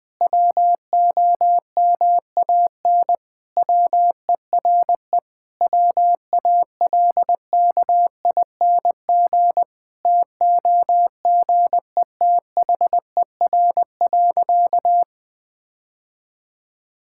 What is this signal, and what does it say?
Morse code, heard as a keyed tone, about 20 wpm, 700 Hz. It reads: WOMAN WERE WALKING TOGETHER.
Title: Sentences with 5-Word Phrases - 20wpm